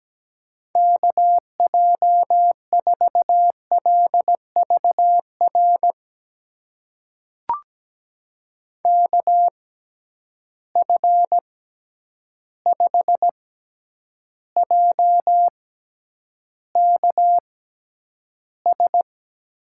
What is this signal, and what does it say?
Morse code, heard as a keyed tone, about 17 wpm, spaced as for 9 wpm, 700 Hz. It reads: KJ4LVR KF5JKS
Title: USA Call Signs - 6 Characters - 6x Character Spacing - 17wpm